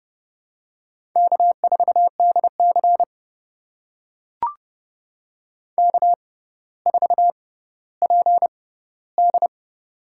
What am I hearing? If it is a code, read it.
Morse code, heard as a keyed tone, about 29 wpm, spaced as for 15 wpm, 700 Hz. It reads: K4DC K4PD
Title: USA Call Signs - 4 Characters - 6x Character Spacing - 30wpm